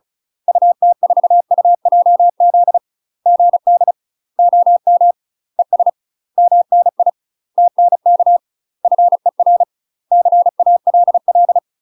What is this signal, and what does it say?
Morse code, heard as a keyed tone, about 35 wpm, 700 Hz. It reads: UT4UJZ GD OM ES MNI TNK FER CALL
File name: UK MASTER QSOs TRAINING / 35WPM / UK MASTER QSOs 0022